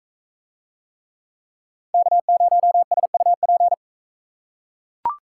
Morse code, heard as a keyed tone, about 41 wpm, 700 Hz. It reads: K0SUP